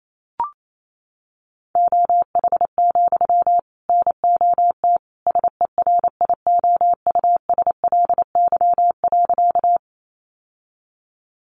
Morse code, heard as a keyed tone, about 28 wpm, 700 Hz. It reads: OH, NOT SERIOUSLY.
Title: Sentences from Top 1200 Words 28wpm